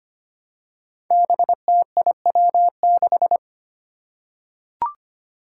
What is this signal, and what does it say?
Morse code, heard as a keyed tone, about 25 wpm, 700 Hz. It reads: BTIW6
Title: Five Challenging Letter-Number Combinations 25wpm